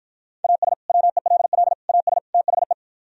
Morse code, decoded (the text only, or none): AS WELL AS THE